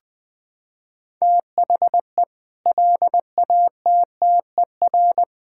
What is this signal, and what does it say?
Morse code, heard as a keyed tone, about 20 wpm, 700 Hz. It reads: THE LATTER
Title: Sentences from Top 1600 Words 20wpm